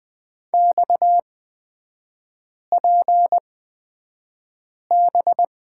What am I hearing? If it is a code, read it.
Morse code, heard as a keyed tone, about 20 wpm, spaced as for 5 wpm, 700 Hz. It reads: XPB
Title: 250 Lesson 40e - BK - Three Characters 20wpm